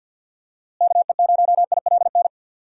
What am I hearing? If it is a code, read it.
Morse code, heard as a keyed tone, about 51 wpm, 700 Hz. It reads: KE0ILN
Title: USA Call Signs - 6 Characters - Repeated 2x - 50wpm